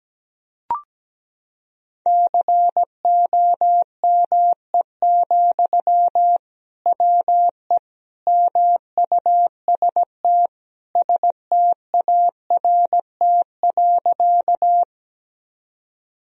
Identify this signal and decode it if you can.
Morse code, heard as a keyed tone, about 17 wpm, 700 Hz. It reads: COME, WE MUST START.
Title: Sentences from Top 1000 Words - Review Part 1 - 17wpm